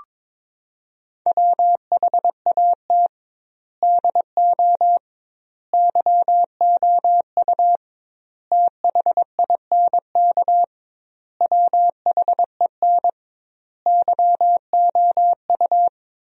Morse code, heard as a keyed tone, about 22 wpm, 700 Hz. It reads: WHAT DO YOU THINK WHEN YOU
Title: 2x Word-Spacing - Sentences from Top 300 Words - 22wpm